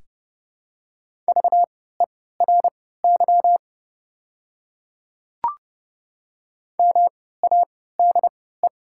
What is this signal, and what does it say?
Morse code, heard as a keyed tone, about 30 wpm, spaced as for 17 wpm, 700 Hz. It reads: VERY MADE
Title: 3x Character Spacing - Top 300 Words - 30wpm